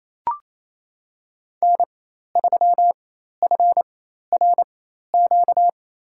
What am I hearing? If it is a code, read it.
Morse code, heard as a keyed tone, about 28 wpm, spaced as for 13 wpm, 700 Hz. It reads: N3FRQ
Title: USA Call Signs - 4x Character Spacing - 28wpm